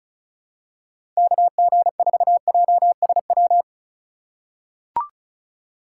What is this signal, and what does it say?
Morse code, heard as a keyed tone, about 34 wpm, 700 Hz. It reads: KG4JSW
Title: USA Call Signs - 6x Character Spacing - 35wpm